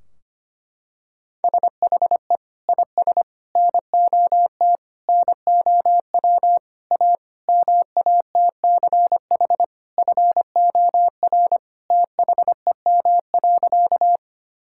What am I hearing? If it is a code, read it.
Morse code, heard as a keyed tone, about 25 wpm, 700 Hz. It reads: SHE IS NOT NOW A MATCH FOR THEM.
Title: Sentences from Top 1600 Words 25wpm